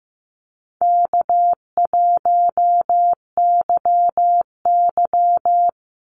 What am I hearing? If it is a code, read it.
Morse code, heard as a keyed tone, about 15 wpm, 700 Hz. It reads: K1YY
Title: Easy Call Signs Repeated 2x - 15wpm